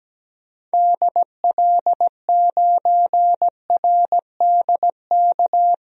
Morse code, heard as a keyed tone, about 17 wpm, 700 Hz. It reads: DL9RDK
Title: DXCC-100 Call Signs - 6 Characters - Repeated 3x - 17wpm